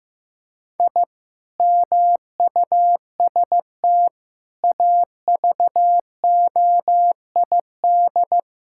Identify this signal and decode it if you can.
Morse code, heard as a keyed tone, about 16 wpm, 700 Hz. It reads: I MUST AVOID
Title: Sentences from Top 1000 Words - Review Part 6 - 15wpm